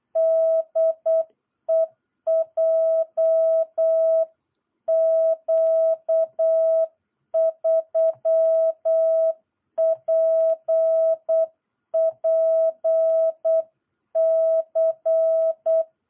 Morse code, heard as a keyed tone, about 8 wpm, 660 Hz. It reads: DE JQ3PPC
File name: jh8cbh / 2022cw